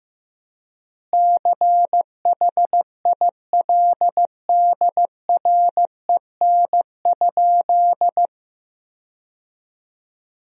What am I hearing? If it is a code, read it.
Morse code, heard as a keyed tone, about 15 wpm, 700 Hz. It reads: CHILDREN?
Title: 4x Word-Spacing - Sentences from Top 300 Words - 15wpm